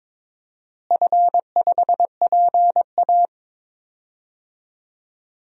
Morse code, heard as a keyed tone, about 22 wpm, 700 Hz. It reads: F5PA
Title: DXCC-100 Call Signs - 4 Characters - Repeated 2x - 22wpm